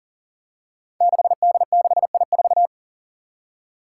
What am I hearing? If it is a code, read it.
Morse code, heard as a keyed tone, about 42 wpm, 700 Hz. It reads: 6D6I4
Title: Five Challenging Letter-Number Combinations 40wpm